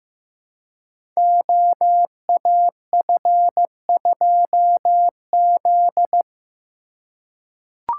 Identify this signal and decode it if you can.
Morse code, heard as a keyed tone, about 15 wpm, 700 Hz. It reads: OAF2Z
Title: Code Groups - 5 Letter:Numbers - 15wpm